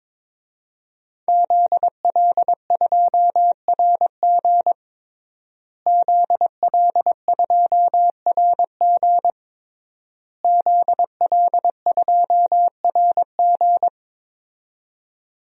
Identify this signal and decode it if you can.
Morse code, heard as a keyed tone, about 22 wpm, 700 Hz. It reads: ZL2RG ZL2RG ZL2RG
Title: DXCC-100 Call Signs - 5 Characters - Repeated 3x - 22wpm